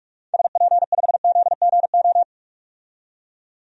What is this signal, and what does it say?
Morse code, heard as a keyed tone, about 46 wpm, 700 Hz. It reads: SP5ZGO
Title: DXCC-100 Call Signs - 6 Characters - 45wpm